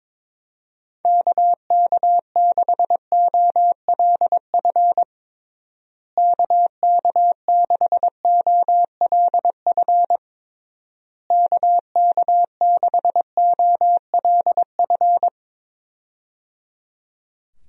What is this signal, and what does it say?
Morse code, heard as a keyed tone, about 22 wpm, 700 Hz. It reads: KK6OLF KK6OLF KK6OLF
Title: USA Call Signs - 6 Characters - Repeated 3x - 22wpm